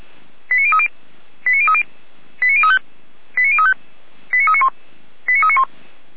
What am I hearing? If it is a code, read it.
Five-tone selective call (ZVEI), heard as quick runs of five tones, tones of about 70 ms. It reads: 89030 89030 89035 89035 89391 89391
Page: SDR/ZVEI